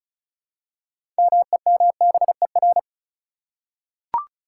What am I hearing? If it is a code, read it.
Morse code, heard as a keyed tone, about 36 wpm, 700 Hz. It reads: MEMBER
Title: Six Letter Words - 35wpm